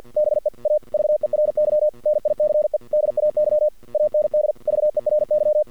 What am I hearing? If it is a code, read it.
Morse code, heard as a keyed tone, about 48 wpm, 600 Hz. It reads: BE A VERY NICE DAY AND VERY
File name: archived / CRSnet